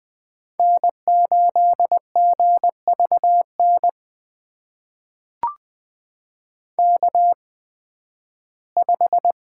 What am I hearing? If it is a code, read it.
Morse code, heard as a keyed tone, about 21 wpm, spaced as for 13 wpm, 700 Hz. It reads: N8GVN K5
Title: USA Call Signs - 5 Characters - 8x Character Spacing - 20wpm